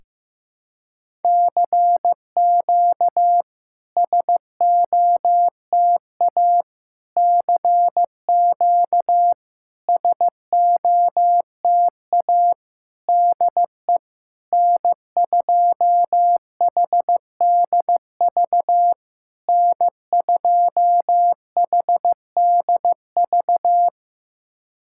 Morse code, heard as a keyed tone, about 15 wpm, 700 Hz. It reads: CQ SOTA CQ SOTA DE N2HDV N2HDV
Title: SOTA QSO Practice 15wpm